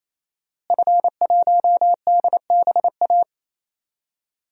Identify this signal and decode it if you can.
Morse code, heard as a keyed tone, about 28 wpm, 700 Hz. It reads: F1DBA
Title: DXCC-100 Call Signs - 5 Characters - Repeated 3x - 28wpm